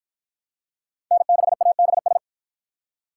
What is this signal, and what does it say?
Morse code, heard as a keyed tone, about 51 wpm, 700 Hz. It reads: N6ABS